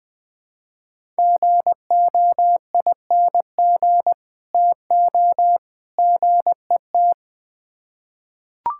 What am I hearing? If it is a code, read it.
Morse code, heard as a keyed tone, about 20 wpm, 700 Hz. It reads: GOING TO GET
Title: Build A Phrase - 3-Word Phrases - 20wpm